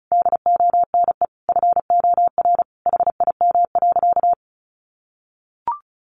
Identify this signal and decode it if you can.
Morse code, heard as a keyed tone, about 34 wpm, 700 Hz. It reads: DONE FOR HIM.
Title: Sentences from Top 1400 Words 35wpm